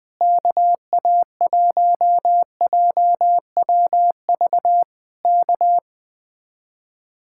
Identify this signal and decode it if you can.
Morse code, heard as a keyed tone, about 20 wpm, 700 Hz. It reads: KA1JWV K